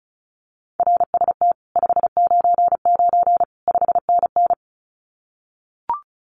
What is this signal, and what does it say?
Morse code, heard as a keyed tone, about 34 wpm, 700 Hz. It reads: RST 599 5NN
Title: Common QSO Phrases Encore 35wpm